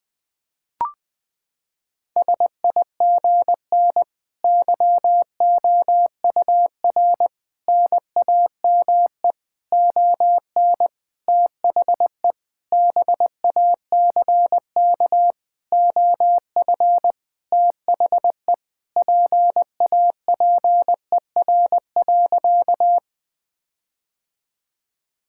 Morse code, heard as a keyed tone, about 20 wpm, 700 Hz. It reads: SIGN YOUR NAME ON THE BACK OF THE PAPER.
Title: Sentences with 5-Word Phrases - 20wpm